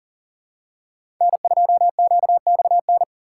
Morse code, heard as a keyed tone, about 37 wpm, 700 Hz. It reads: N2QXN
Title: USA Call Signs - 5 Characters - 6x Character Spacing - 40wpm